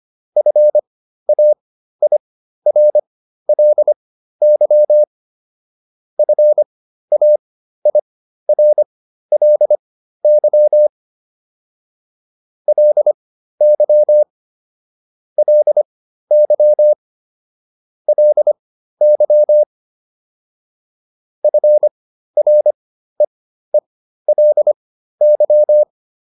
Morse code, heard as a keyed tone, about 25 wpm, spaced as for 13 wpm, 600 Hz. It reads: FAIRLY FAIRLY LY LY LY FREELY